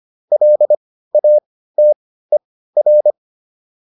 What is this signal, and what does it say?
Morse code, heard as a keyed tone, about 25 wpm, spaced as for 15 wpm, 600 Hz. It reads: LATER